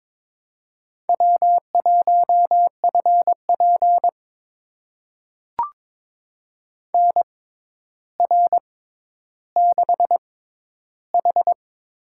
Morse code, heard as a keyed tone, about 22 wpm, spaced as for 11 wpm, 700 Hz. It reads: W1FP NR6H
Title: USA Call Signs - 4 Characters - 6x Character Spacing - 22wpm